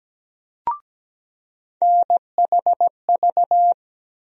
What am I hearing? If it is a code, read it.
Morse code, heard as a keyed tone, about 17 wpm, 700 Hz. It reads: NHV
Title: USA Call Sign Postfixes - 3 Letters 17wpm